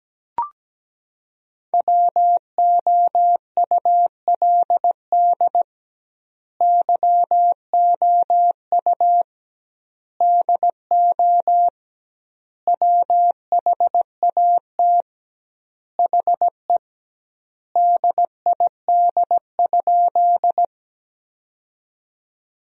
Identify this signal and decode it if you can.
Morse code, heard as a keyed tone, about 17 wpm, 700 Hz. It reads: WOULD YOU DO WHAT HE DID?
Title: 2x Word-Spacing - Sentences from Top 100 Words - 17wpm